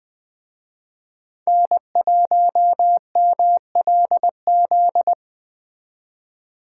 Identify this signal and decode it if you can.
Morse code, heard as a keyed tone, about 20 wpm, 700 Hz. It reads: N1MLZ
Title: Speed Racing - USA Call Signs 17wpm